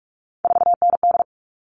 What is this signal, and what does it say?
Morse code, heard as a keyed tone, about 44 wpm, 700 Hz. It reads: 4ND
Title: All Cut Numbers - 3-Digit 45wpm